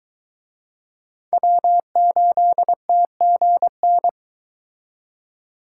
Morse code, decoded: W8TGN

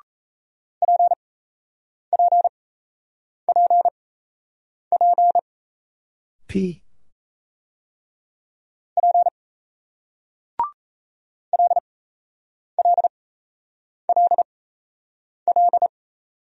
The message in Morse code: PPPP P LLLL

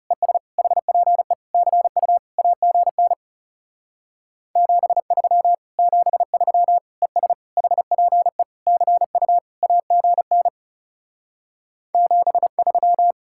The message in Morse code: ES HPE CU AGN 73 73 ES HPE CU AGN 73

